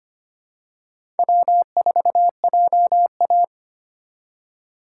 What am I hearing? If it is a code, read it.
Morse code, heard as a keyed tone, about 25 wpm, 700 Hz. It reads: W4JA
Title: USA Call Signs - 4x Character Spacing - 25wpm